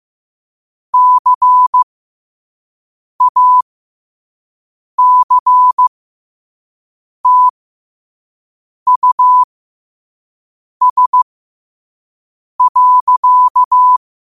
Morse code, heard as a keyed tone, about 15 wpm, spaced as for 5 wpm, 1000 Hz. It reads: CACTUS.
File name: qotdfiles / 5wpm